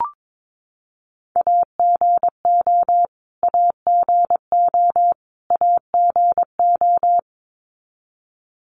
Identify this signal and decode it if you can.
Morse code, heard as a keyed tone, about 22 wpm, 700 Hz. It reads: AGO AGO AGO
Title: Top 500 Words Repeated 22wpm